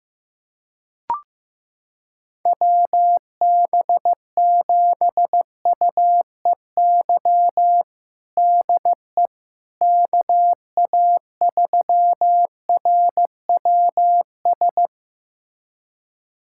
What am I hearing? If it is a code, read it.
Morse code, heard as a keyed tone, about 15 wpm, 700 Hz. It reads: WB7UEY DE KA3RWS
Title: USA Callsign DE Callsign - 6 Characters - 15wpm